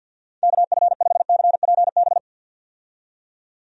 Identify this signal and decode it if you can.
Morse code, heard as a keyed tone, about 51 wpm, 700 Hz. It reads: KF5CPB
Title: USA Call Signs - 6 Characters - 50wpm